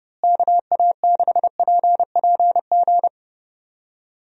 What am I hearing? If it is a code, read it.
Morse code, heard as a keyed tone, about 30 wpm, 700 Hz. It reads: KA6PPG
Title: USA Call Signs - 6 Characters - Repeated 2x - 30wpm